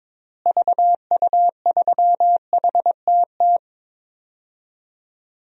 Morse code, heard as a keyed tone, about 21 wpm, 700 Hz. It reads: VU3HTT
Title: DXCC-100 Call Signs - 6 Characters - Repeated 2x - 22wpm